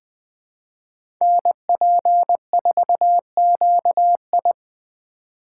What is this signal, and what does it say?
Morse code, heard as a keyed tone, about 21 wpm, 700 Hz. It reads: NP4QI
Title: USA Call Signs - 4x Character Spacing - 20wpm